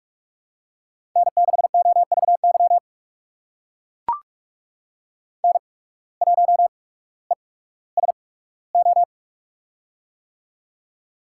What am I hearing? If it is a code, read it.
Morse code, heard as a keyed tone, about 44 wpm, spaced as for 19 wpm, 700 Hz. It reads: N6OVY N1ESO
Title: USA Call Signs - 5 Characters - 8x Character Spacing - 45wpm